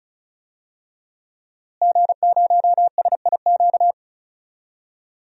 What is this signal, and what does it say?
Morse code, heard as a keyed tone, about 36 wpm, 700 Hz. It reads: G0SIQ
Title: DXCC-100 Call Signs - 5 Characters - Repeated 2x - 35wpm